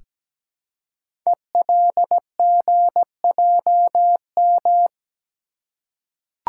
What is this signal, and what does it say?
Morse code, heard as a keyed tone, about 17 wpm, 700 Hz. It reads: ELGJM
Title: Code Groups - 5 Letters - 17wpm